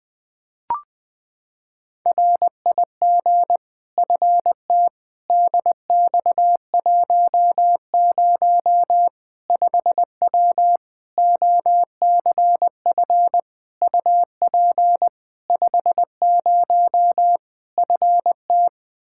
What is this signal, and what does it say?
Morse code, heard as a keyed tone, about 20 wpm, 700 Hz. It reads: RIG FT DX10 5W OCF UP 50 FT
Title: Practice QSO - Exchange 2 - Info - 20wpm